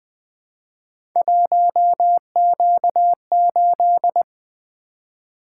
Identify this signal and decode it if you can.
Morse code, heard as a keyed tone, about 20 wpm, 700 Hz. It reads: 1Q8
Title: Three Letter-Number Combinations - 20wpm